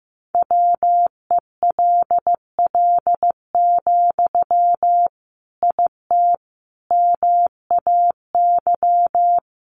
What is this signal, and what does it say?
Morse code, heard as a keyed tone, about 14 wpm, 700 Hz. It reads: WELL, IT MAY
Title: Sentences from Top 1500 Words 15wpm